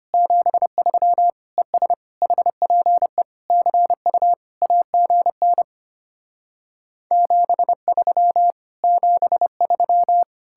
Morse code, heard as a keyed tone, about 33 wpm, 700 Hz. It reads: 73 ES HPE CU AGN 73 73